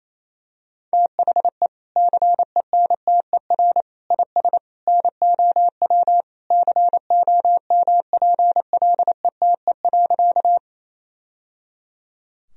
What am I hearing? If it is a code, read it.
Morse code, heard as a keyed tone, about 28 wpm, 700 Hz. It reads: THE CENTER IS NOW COMPLETE.
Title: Sentences from Top 1000 Words - Review Part 2 - 28wpm